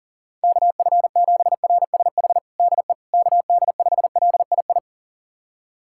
Morse code, heard as a keyed tone, about 40 wpm, 700 Hz. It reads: KF7RSH DE KD5LII
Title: USA Callsign DE Callsign 40wpm